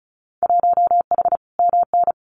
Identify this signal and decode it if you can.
Morse code, heard as a keyed tone, about 36 wpm, 700 Hz. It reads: 1H MN
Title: Winter Field Day Exchanges Encore 35wpm